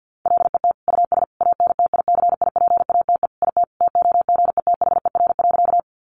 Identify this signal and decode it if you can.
Morse code, heard as a keyed tone, about 50 wpm, 700 Hz. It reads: LET US ANTICIPATE IT TOGETHER.
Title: Sentences from Top 1500 Words 50wpm